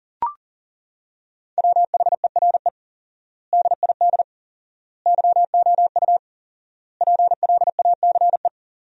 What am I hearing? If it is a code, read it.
Morse code, heard as a keyed tone, about 40 wpm, 700 Hz. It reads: WHERE DID YOU PLACE
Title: 4x Word-Spacing - Sentences from Top 200 Words - 40wpm